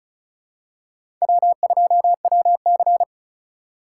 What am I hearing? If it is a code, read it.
Morse code, heard as a keyed tone, about 36 wpm, 700 Hz. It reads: W2WC
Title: USA Call Signs - 4 Characters - 3x Character Spacing - 35wpm